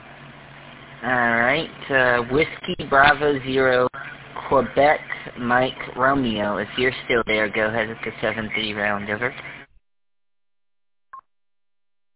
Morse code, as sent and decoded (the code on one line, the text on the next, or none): none